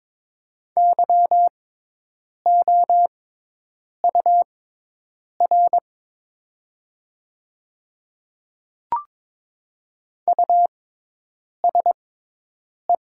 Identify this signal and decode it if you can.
Morse code, heard as a keyed tone, about 21 wpm, spaced as for 8 wpm, 700 Hz. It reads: YOUR USE
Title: 6x Character Spacing - Top 100 Words - 22wpm